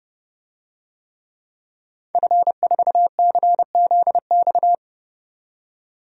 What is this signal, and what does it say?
Morse code, heard as a keyed tone, about 31 wpm, 700 Hz. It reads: F4CZX
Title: DXCC-100 Call Signs - 5 Characters - 30wpm